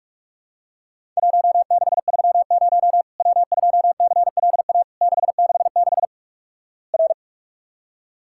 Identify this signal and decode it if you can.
Morse code, heard as a keyed tone, about 46 wpm, 700 Hz. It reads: 1630 W2CLA 666 R
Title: Worked All Europe DX Contest - QTC Exchange - 45wpm